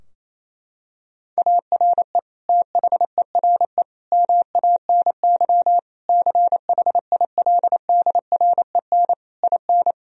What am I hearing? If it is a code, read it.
Morse code, heard as a keyed tone, about 28 wpm, 700 Hz. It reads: ARE THERE MANY CHILDREN IN
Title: Sentences from Top 1000 Words - Review Part 4 - 28wpm